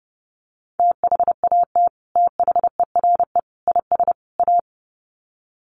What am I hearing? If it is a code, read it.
Morse code, heard as a keyed tone, about 30 wpm, 700 Hz. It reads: THAT THERE IS A
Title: Sets of 4 Words 30wpm